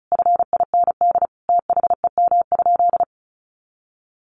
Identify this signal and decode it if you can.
Morse code, heard as a keyed tone, about 34 wpm, 700 Hz. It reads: FIND THEM?
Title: Sentences from Top 200 Words 35wpm